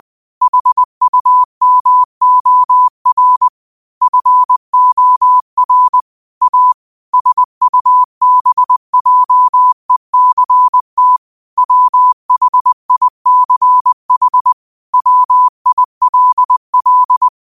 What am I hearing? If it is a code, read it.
Morse code, heard as a keyed tone, about 20 wpm, 1000 Hz. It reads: HUMOR FOR A SUBJECT WHICH WILL
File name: qotdfiles / 20wpm